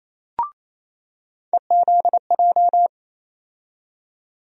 Morse code, heard as a keyed tone, about 28 wpm, 700 Hz. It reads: EZJ
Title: USA Call Sign Postfixes - 3 Letters 28wpm